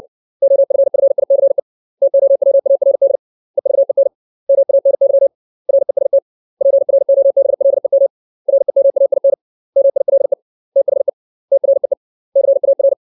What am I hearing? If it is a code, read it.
Morse code, heard as a keyed tone, about 60 wpm, 550 Hz. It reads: OFFICE TOWARD EVEN MANY REST PROBLEM REMAIN MILE THE TREE CAR
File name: 632 words / 632 Words 60WPM